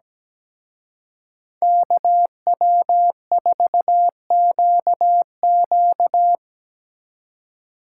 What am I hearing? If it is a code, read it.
Morse code, heard as a keyed tone, about 20 wpm, 700 Hz. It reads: KW4QQ
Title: Speed Racing - USA Call Signs 17wpm